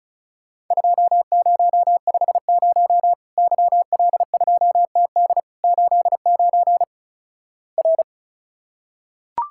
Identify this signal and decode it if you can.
Morse code, heard as a keyed tone, about 36 wpm, 700 Hz. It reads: 2050 YL2TD 89 R
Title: Worked All Europe DX Contest - QTC Exchange - 35wpm